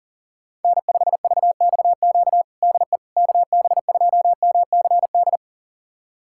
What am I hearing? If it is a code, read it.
Morse code, heard as a keyed tone, about 40 wpm, 700 Hz. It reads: N5VXQ DE KB2MCD
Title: USA Callsign DE Callsign 40wpm